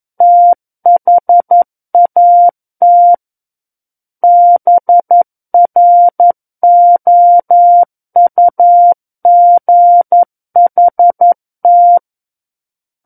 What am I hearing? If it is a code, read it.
Morse code, heard as a keyed tone, about 11 wpm, 700 Hz. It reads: THAT BROUGHT
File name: Book One / 02-Chapter 2 - 11 WPM